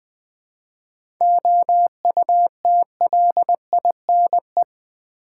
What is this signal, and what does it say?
Morse code, heard as a keyed tone, about 20 wpm, 700 Hz. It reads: OUTLINE
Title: 7 Letter Words - First 3 Letters Given - 20wpm